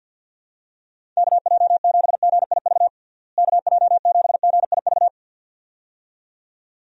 Morse code, heard as a keyed tone, about 53 wpm, 700 Hz. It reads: KJ7GIV KJ7GIV
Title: USA Call Signs - 6 Characters - Repeated 2x - 50wpm